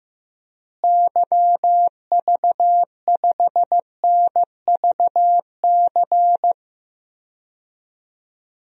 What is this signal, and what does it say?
Morse code, heard as a keyed tone, about 15 wpm, 700 Hz. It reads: YV5NVC
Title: DXCC-100 Call Signs - 6 Characters - Repeated 3x - 15wpm